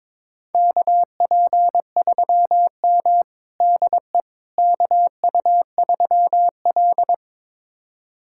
Morse code, heard as a keyed tone, about 22 wpm, 700 Hz. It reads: KP3M DE KU3L
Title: USA Callsign DE Callsign - 4 Characters - 22wpm